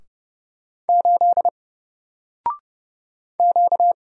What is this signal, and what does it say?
Morse code, heard as a keyed tone, about 30 wpm, 700 Hz. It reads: 8 Q
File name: Single-Letter-Number-30wpm